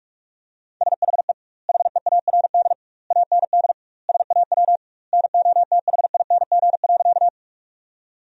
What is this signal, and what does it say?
Morse code, heard as a keyed tone, about 42 wpm, 700 Hz. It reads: SHE HEARD AND SAW NOTHING.
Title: 2x Word-Spacing - Sentences from Top 500 Words - 45wpm